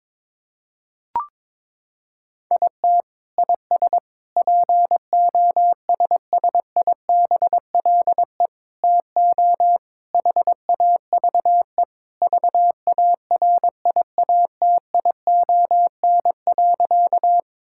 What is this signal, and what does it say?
Morse code, heard as a keyed tone, about 21 wpm, 700 Hz. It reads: IT IS POSSIBLE TO HAVE VARIATION.
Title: Sentences from Top 1600 Words 22wpm